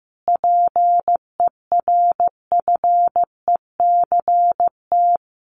Morse code, heard as a keyed tone, about 15 wpm, 700 Hz. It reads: PERFECT
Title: Sentences from Top 1000 Words 15wpm